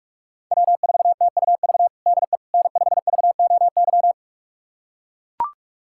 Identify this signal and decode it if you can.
Morse code, heard as a keyed tone, about 42 wpm, 700 Hz. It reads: W4TUV DE N5VOY
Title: USA Callsign DE Callsign - 5 Characters - 45wpm